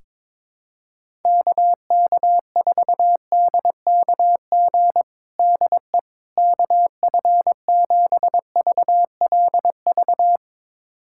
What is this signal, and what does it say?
Morse code, heard as a keyed tone, about 22 wpm, 700 Hz. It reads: KK4DKG DE KF7VLV
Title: USA Callsign DE Callsign 22wpm